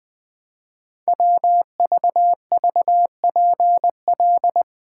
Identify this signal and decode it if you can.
Morse code, heard as a keyed tone, about 20 wpm, 700 Hz. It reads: WVVPL
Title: Code Groups - 5 Letter:Numbers - 20wpm